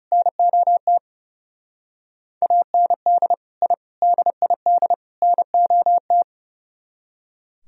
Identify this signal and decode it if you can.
Morse code, heard as a keyed tone, about 37 wpm, 700 Hz. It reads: NOT AND I DID NOT
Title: Speed-Racing - Sets of 4 Words 30wpm